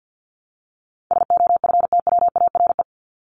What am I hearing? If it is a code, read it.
Morse code, heard as a keyed tone, about 50 wpm, 700 Hz. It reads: SOFTWARE
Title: Top 501-750 Words 50wpm